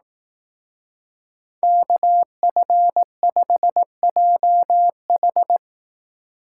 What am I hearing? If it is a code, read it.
Morse code, heard as a keyed tone, about 18 wpm, 700 Hz. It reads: KF5JH
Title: Speed Racing - USA 2x2 Call Signs 15wpm